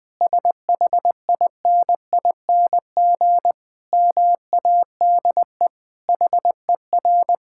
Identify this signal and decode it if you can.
Morse code, heard as a keyed tone, about 21 wpm, 700 Hz. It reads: SHINING MADE HER
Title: Sentences with 4-Word Phrases - 20wpm